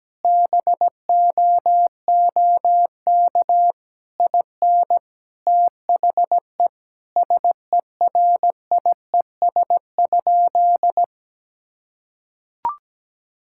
Morse code, heard as a keyed tone, about 17 wpm, 700 Hz. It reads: BOOK IN THE SERIES?
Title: Sentences from Top 1000 Words - Review Part 5 - 17wpm